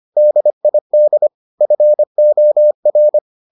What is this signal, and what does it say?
Morse code, heard as a keyed tone, about 25 wpm, 600 Hz. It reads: DID FOR